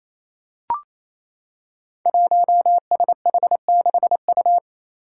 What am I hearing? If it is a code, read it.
Morse code, heard as a keyed tone, about 27 wpm, 700 Hz. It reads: 1SH6U
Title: Five Challenging Letter-Number Combinations 28wpm